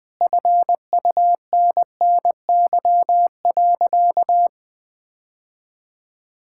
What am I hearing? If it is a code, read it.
Morse code, heard as a keyed tone, about 20 wpm, 700 Hz. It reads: FUNNY.
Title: Sentences with 2-Word Phrases - Spoken Beginning - 20wpm